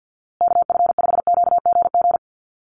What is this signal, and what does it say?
Morse code, heard as a keyed tone, about 48 wpm, 700 Hz. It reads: KF5QGG
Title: USA Call Signs Repeated 2x 50wpm